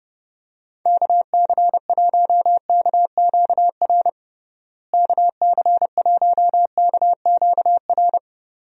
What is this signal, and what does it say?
Morse code, heard as a keyed tone, about 30 wpm, 700 Hz. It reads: KC1KQR KC1KQR
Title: USA Call Signs - 6 Characters - Repeated 2x - 30wpm